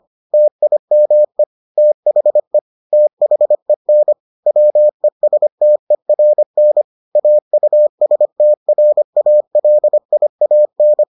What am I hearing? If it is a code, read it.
Morse code, heard as a keyed tone, about 25 wpm, 600 Hz. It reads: TIME THE THEN WESTERN AUSTRALIAN